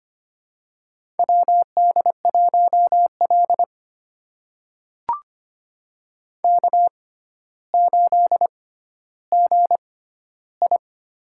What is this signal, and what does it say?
Morse code, heard as a keyed tone, about 25 wpm, spaced as for 13 wpm, 700 Hz. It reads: WD1L K8GI
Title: USA Call Signs - 4 Characters - 6x Character Spacing - 25wpm